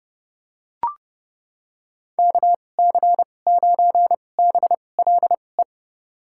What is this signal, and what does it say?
Morse code, heard as a keyed tone, about 30 wpm, spaced as for 22 wpm, 700 Hz. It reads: KC9BLE
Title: USA Call Signs - 6 Characters - 2x Character Spacing - 30wpm